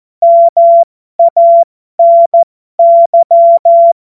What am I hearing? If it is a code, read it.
Morse code, heard as a keyed tone, about 14 wpm, spaced as for 12 wpm, 680 Hz. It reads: MANY
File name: Morsekurs / Witze